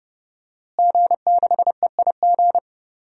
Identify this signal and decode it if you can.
Morse code, heard as a keyed tone, about 31 wpm, 700 Hz. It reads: G6EIG
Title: DXCC-100 Call Signs - 5 Characters - Repeated 3x - 30wpm